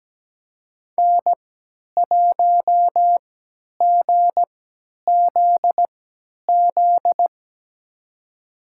Morse code, heard as a keyed tone, about 17 wpm, spaced as for 10 wpm, 700 Hz. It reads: N1GZZ
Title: USA Call Signs - 5 Characters - 3x Character Spacing - 17wpm